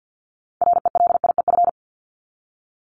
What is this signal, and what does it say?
Morse code, heard as a keyed tone, about 50 wpm, 700 Hz. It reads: RELIEF
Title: Top 1501-1750 Words 50wpm